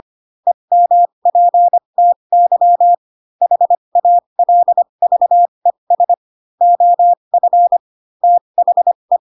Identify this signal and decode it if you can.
Morse code, heard as a keyed tone, about 25 wpm, spaced as for 22 wpm, 700 Hz. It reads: EMPTY HALVES OF THE